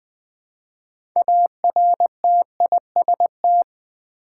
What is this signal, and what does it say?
Morse code, heard as a keyed tone, about 20 wpm, 700 Hz. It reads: ARTIST
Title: 6 Letter Words - First 3 Letters Given - 20wpm